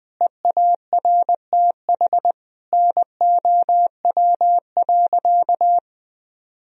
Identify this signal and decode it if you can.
Morse code, heard as a keyed tone, about 21 wpm, 700 Hz. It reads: EARTH NOW.
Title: Sentences from Top 1000 Words - Review Part 2 - 20wpm